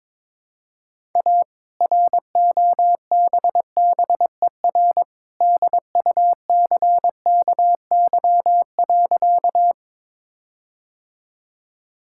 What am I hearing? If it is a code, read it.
Morse code, heard as a keyed tone, about 22 wpm, 690 Hz. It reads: A ROBBER DUCKY.
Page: morse-code-ninja-practice-22wpm